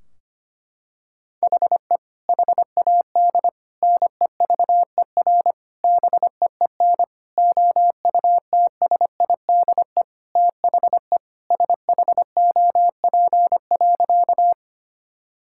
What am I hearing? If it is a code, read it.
Morse code, heard as a keyed tone, about 25 wpm, 700 Hz. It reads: HE HAD NEVER BEEN OUTSIDE THE SHOP.
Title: Sentences from Top 1600 Words 25wpm